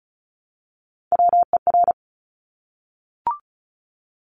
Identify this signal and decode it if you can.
Morse code, heard as a keyed tone, about 36 wpm, 700 Hz. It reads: WER
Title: Top Three Letter Combinations in Words 35wpm